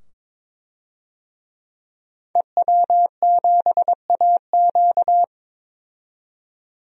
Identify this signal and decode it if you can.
Morse code, heard as a keyed tone, about 22 wpm, 700 Hz. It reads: EW7AQ
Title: DXCC-100 Call Signs - 5 Characters - 22wpm